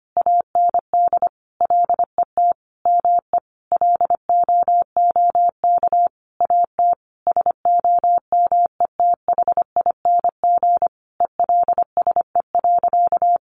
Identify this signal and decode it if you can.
Morse code, heard as a keyed tone, about 25 wpm, 700 Hz. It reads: AND LET ME LOOK AT SOMETHING ELSE.
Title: Sentences from Top 800 Words 25wpm